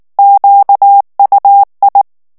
Morse code, heard as a keyed tone, about 19 wpm, 800 Hz. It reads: QUI
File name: FRANCAIS TEXTE CLAIR 2 VITESSE 18 MOTS MINUTES